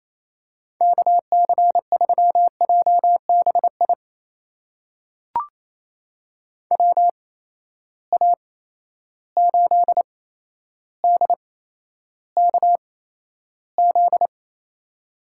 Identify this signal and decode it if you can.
Morse code, heard as a keyed tone, about 28 wpm, spaced as for 12 wpm, 700 Hz. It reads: KC3JBI WA8DKZ
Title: USA Call Signs - 6 Characters - 8x Character Spacing - 28wpm